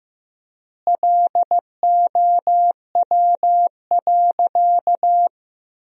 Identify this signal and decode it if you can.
Morse code, heard as a keyed tone, about 15 wpm, 700 Hz. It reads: LOW.
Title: 4x Word-Spacing - Sentences from Top 500 Words - 15wpm